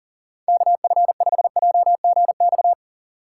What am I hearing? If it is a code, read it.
Morse code, heard as a keyed tone, about 40 wpm, 700 Hz. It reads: KF5JGX
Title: USA Call Signs - 6 Characters - 6x Character Spacing - 40wpm